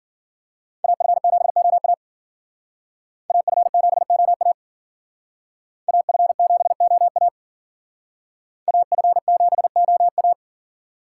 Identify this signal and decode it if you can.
Morse code, heard as a keyed tone, about 54 wpm, 700 Hz. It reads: AF7OA AF7OA AF7OA AF7OA